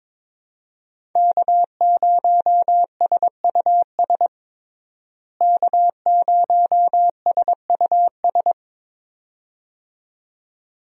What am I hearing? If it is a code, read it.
Morse code, heard as a keyed tone, about 22 wpm, 700 Hz. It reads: K0SUS K0SUS
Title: USA Call Signs - 5 Characters - Repeated 2x - 22wpm